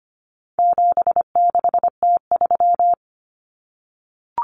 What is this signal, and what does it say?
Morse code, heard as a keyed tone, about 25 wpm, 700 Hz. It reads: 76T3
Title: Common Cut Numbers - 4-Digit 25wpm